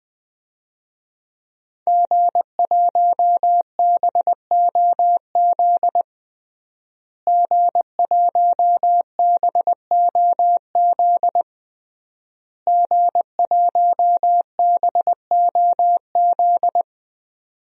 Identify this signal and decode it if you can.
Morse code, heard as a keyed tone, about 20 wpm, 700 Hz. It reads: G1BOZ G1BOZ G1BOZ
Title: DXCC-100 Call Signs - 5 Characters - Repeated 3x - 20wpm